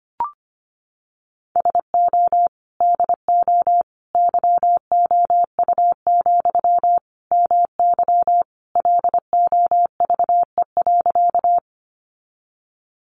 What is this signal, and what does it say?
Morse code, heard as a keyed tone, about 25 wpm, 700 Hz. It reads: SO DO YOU, MY LOVE.
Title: Sentences from Top 400 Words 25wpm